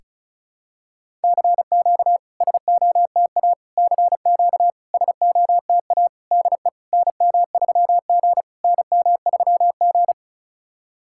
Morse code, heard as a keyed tone, about 36 wpm, 700 Hz. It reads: CQ SOTA CQ SOTA DE NM3G NM3G
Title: SOTA QSO Practice 35wpm